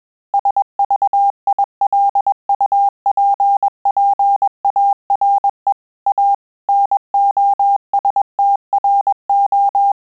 Morse code, heard as a keyed tone, about 21 wpm, 800 Hz. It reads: SVILUPPARE A NOSTRO
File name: Telegraphy-group / breve testo midi